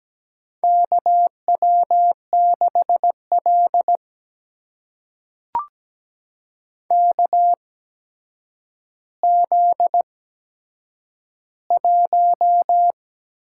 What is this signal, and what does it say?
Morse code, heard as a keyed tone, about 17 wpm, spaced as for 8 wpm, 700 Hz. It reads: KW6L KZ1
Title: USA Call Signs - 4 Characters - 8x Character Spacing - 17wpm